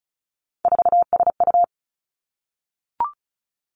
Morse code, heard as a keyed tone, about 36 wpm, 700 Hz. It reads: 4SU
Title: Three Challenging Letter-Number Combinations 35wpm